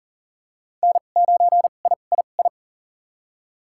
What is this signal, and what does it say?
Morse code, heard as a keyed tone, about 40 wpm, spaced as for 29 wpm, 700 Hz. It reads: N9III
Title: USA Call Signs - 2x Character Spacing - 40wpm